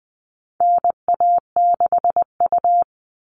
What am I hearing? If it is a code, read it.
Morse code, heard as a keyed tone, about 20 wpm, 700 Hz. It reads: NA6U
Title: All Cut Numbers - 4-Digit 20wpm